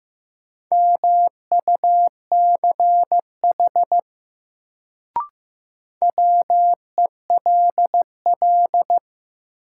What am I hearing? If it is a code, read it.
Morse code, heard as a keyed tone, about 15 wpm, 700 Hz. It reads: MUCH WELL
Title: Sets of 2 Words - New Word Drill - 15wpm